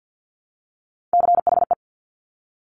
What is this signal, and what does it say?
Morse code, heard as a keyed tone, about 50 wpm, 700 Hz. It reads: CHE
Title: Top Three Letter Combinations in Words 50wpm